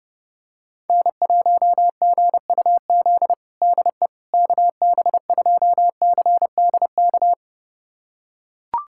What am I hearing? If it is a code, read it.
Morse code, heard as a keyed tone, about 30 wpm, 700 Hz. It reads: N1GUZ DE KB2CDK
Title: USA Callsign DE Callsign 30wpm